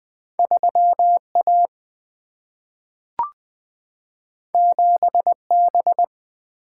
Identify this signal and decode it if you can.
Morse code, heard as a keyed tone, about 20 wpm, 700 Hz. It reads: 3 A 7 B
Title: Single-Letter Single-Number 20wpm